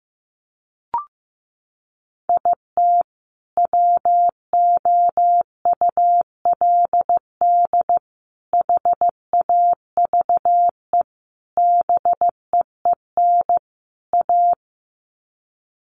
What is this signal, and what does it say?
Morse code, heard as a keyed tone, about 15 wpm, 700 Hz. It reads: IT WOULD HAVE BEEN A
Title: Sets of 5 Words Encore 15wpm